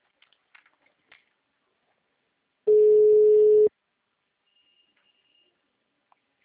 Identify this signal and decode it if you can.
Ringback tone (Europe, 425 Hz).
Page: Itatiba